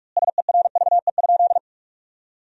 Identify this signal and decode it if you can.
Morse code, heard as a keyed tone, about 46 wpm, 700 Hz. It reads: SERVE?
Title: 2x Word-Spacing - Sentences from Top 400 Words - 45wpm